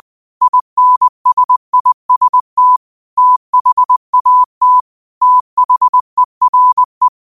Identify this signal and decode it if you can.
Morse code, heard as a keyed tone, about 20 wpm, 1000 Hz. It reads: INSIST THAT THERE
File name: qotdfiles / 20wpm